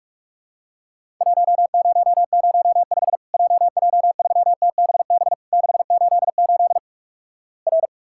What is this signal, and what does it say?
Morse code, heard as a keyed tone, about 46 wpm, 700 Hz. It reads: 1005 JJ3TBB 688 R
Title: Worked All Europe DX Contest - QTC Exchange - 45wpm